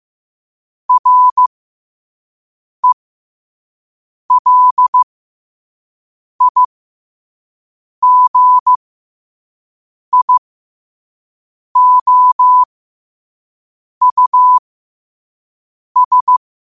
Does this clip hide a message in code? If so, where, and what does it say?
Yes: Morse code from 0.9 to 16.4 s, heard as a keyed tone, about 15 wpm, spaced as for 5 wpm, 1000 Hz. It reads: RELIGIOUS